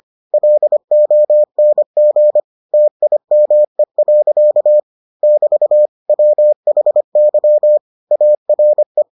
Morse code, heard as a keyed tone, about 25 wpm, 600 Hz. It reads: LONG TIME. <BT> WHY ARE